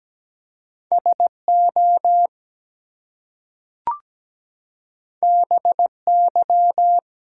Morse code, heard as a keyed tone, about 17 wpm, 700 Hz. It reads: SO BY
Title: Two Letter Words - 17wpm